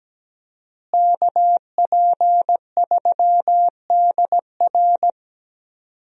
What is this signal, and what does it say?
Morse code, heard as a keyed tone, about 17 wpm, 700 Hz. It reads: KP3DR
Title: USA Call Signs - 5 Characters - Repeated 2x - 17wpm